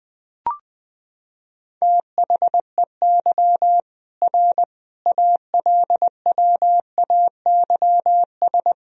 Morse code, heard as a keyed tone, about 21 wpm, 700 Hz. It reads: THEY R ALWAYS
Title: Sentences from Top 300 Words and Abbreviations 20wpm